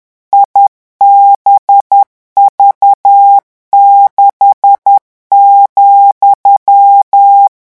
Morse code, heard as a keyed tone, about 10 wpm, 800 Hz. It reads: IBV6,
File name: morse mp3 / revision13-vitesse 10 mots